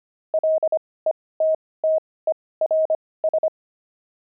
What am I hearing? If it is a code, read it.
Morse code, heard as a keyed tone, about 25 wpm, spaced as for 18 wpm, 630 Hz. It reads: LETTERS
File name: pangram-fs-18-cs-25-jocks